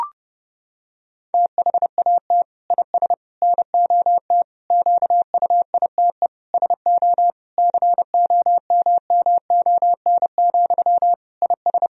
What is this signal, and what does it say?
Morse code, heard as a keyed tone, about 32 wpm, 700 Hz. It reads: THAT IS NOT QUITE SO COMMON, IS
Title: Sentences from Top 1000 Words - Review Part 3 - 30wpm